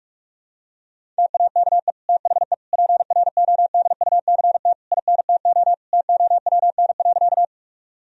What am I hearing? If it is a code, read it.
Morse code, heard as a keyed tone, about 46 wpm, 700 Hz. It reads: TAKE THE PRODUCT INTO TOWN.
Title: Sentences from Top 1000 Words - Review Part 2 - 45wpm